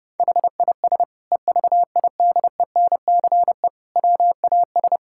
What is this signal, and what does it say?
Morse code, heard as a keyed tone, about 30 wpm, 700 Hz. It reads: HIS EVIDENCE WAS
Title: Sentences from Top 1600 Words 30wpm